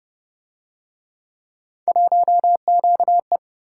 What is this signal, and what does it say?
Morse code, heard as a keyed tone, about 32 wpm, 700 Hz. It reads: 1QE